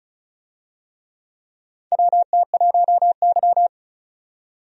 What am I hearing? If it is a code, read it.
Morse code, heard as a keyed tone, about 36 wpm, 700 Hz. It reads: WT1Y
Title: USA Call Signs - 4 Characters - 35wpm